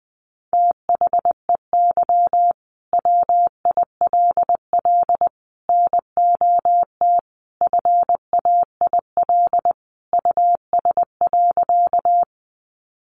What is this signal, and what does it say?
Morse code, heard as a keyed tone, about 20 wpm, 700 Hz. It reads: THEY WILL NOT FAIL US.
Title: Sentences from Top 1100 Words 20wpm